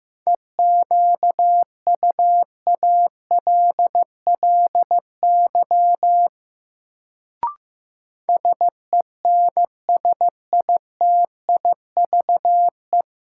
Morse code, heard as a keyed tone, about 15 wpm, 700 Hz. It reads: EQUALLY SENSITIVE
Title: Sentences from Top 1400 Words - New Word Drill Alt - 15wpm